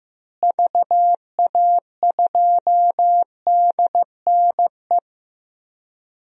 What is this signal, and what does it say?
Morse code, heard as a keyed tone, about 16 wpm, 700 Hz. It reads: VA2DNE
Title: DXCC-100 Call Signs - 6 Characters - Repeated 2x - 15wpm